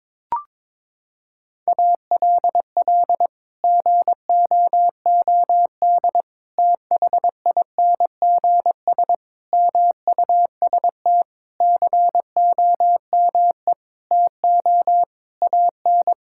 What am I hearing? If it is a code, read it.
Morse code, heard as a keyed tone, about 22 wpm, 700 Hz. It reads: ALL GOOD THINGS MUST COME TO AN
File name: Sets of 9 Words - 22wpm